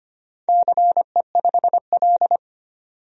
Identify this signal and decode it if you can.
Morse code, heard as a keyed tone, about 25 wpm, 700 Hz. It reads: CE5L
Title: DXCC-100 Call Signs - 4 Characters - 25wpm